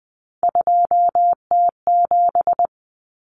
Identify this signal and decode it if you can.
Morse code, heard as a keyed tone, about 20 wpm, 700 Hz. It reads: 2T7
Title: Common Cut Numbers - 3-Digit 20wpm